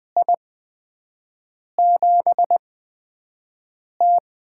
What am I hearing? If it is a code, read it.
Morse code, heard as a keyed tone, about 20 wpm, spaced as for 5 wpm, 700 Hz. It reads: I7T
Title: USA Call Signs - 4 Characters - 8x Character Spacing - 20wpm